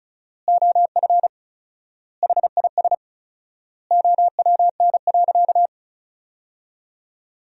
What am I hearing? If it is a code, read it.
Morse code, heard as a keyed tone, about 36 wpm, 700 Hz. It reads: OF HIS OWN.
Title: 4x Word-Spacing - Sentences from Top 300 Words - 35wpm